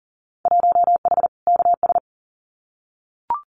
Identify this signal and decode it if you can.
Morse code, heard as a keyed tone, about 40 wpm, 700 Hz. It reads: 1H KS
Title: Winter Field Day Exchanges 40wpm